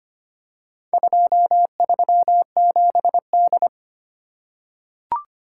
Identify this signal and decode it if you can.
Morse code, heard as a keyed tone, about 25 wpm, 700 Hz. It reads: 237D
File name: practice / 25wpm / Four Letter-Number Combinations - 25wpm